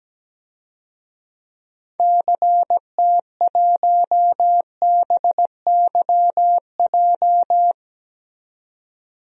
Morse code, heard as a keyed tone, about 17 wpm, 700 Hz. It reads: CT1BYJ
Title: DXCC-100 Call Signs - 6 Characters - 17wpm